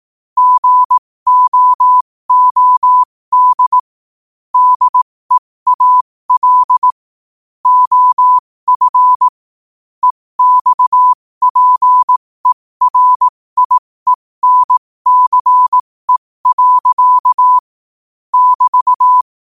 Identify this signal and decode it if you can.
Morse code, heard as a keyed tone, about 18 wpm, spaced as for 15 wpm, 1000 Hz. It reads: GOOD DEAL OF EXPERIENCE. <BT>
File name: qotdfiles / 15wpm